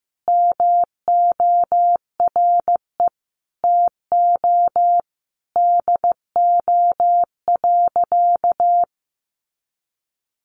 Morse code, heard as a keyed tone, about 15 wpm, 700 Hz. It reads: MORE TO DO.
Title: Sentences from Top 100 Words 15wpm